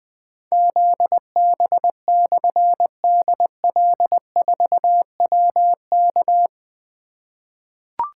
Morse code, heard as a keyed tone, about 20 wpm, 700 Hz. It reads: ZB/DL4WK
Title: Difficult Call Signs 20wpm